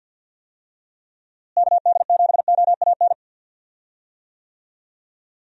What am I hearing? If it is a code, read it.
Morse code, heard as a keyed tone, about 47 wpm, 700 Hz. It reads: KD7OAN
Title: USA Call Signs - 6 Characters - Repeated 3x - 50wpm